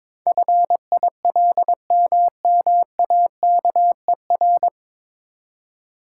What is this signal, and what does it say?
Morse code, heard as a keyed tone, about 22 wpm, 700 Hz. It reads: FILMMAKER
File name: 9 Letter Words - First 3 Letters Given - 22wpm